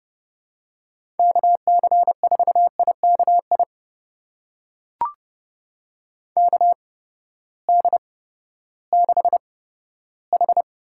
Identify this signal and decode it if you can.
Morse code, heard as a keyed tone, about 30 wpm, spaced as for 15 wpm, 700 Hz. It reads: KC4IKI KD6H
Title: USA Call Signs - 6 Characters - 8x Character Spacing - 30wpm